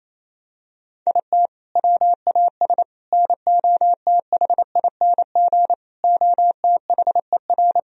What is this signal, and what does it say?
Morse code, heard as a keyed tone, about 28 wpm, 700 Hz. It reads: IT WAS NOTHING OTHER
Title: Sentences from Top 1000 Words - Review Part 2 - 28wpm